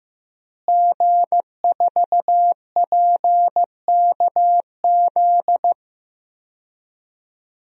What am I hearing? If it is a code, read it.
Morse code, heard as a keyed tone, about 15 wpm, 700 Hz. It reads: G4PKZ
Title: DXCC-100 Call Signs - 5 Characters - Repeated 3x - 15wpm